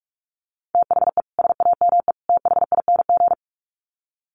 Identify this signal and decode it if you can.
Morse code, heard as a keyed tone, about 44 wpm, 700 Hz. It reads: THE SAME THING
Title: Sets of 3 Words 45wpm